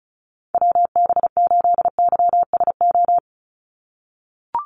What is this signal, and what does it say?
Morse code, heard as a keyed tone, about 36 wpm, 700 Hz. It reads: WB8YSO
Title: USA Call Signs 35wpm